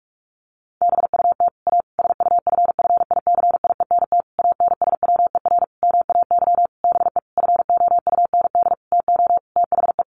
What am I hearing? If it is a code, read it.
Morse code, heard as a keyed tone, about 44 wpm, 700 Hz. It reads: BUT A SUFFICIENT ANSWER MAY BE FOUND TO THE
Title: Sentences from Top 1400 Words 45wpm